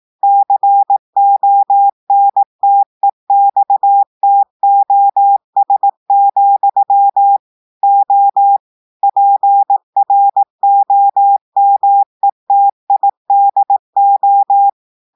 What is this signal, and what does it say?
Morse code, heard as a keyed tone, about 18 wpm, 800 Hz. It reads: CONTEXTOS, O PROMETIDO
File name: audios / portugais / portugais texte en clair 1 vitesse 18 mots minutes